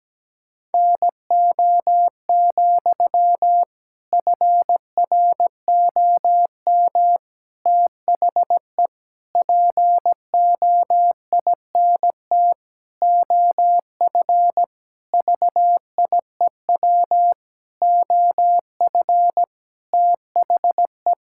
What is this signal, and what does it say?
Morse code, heard as a keyed tone, about 17 wpm, 700 Hz. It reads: NO, FROM THE POINT OF VIEW OF THE
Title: Sentences from Top 1000 Words - Review Part 6 - 17wpm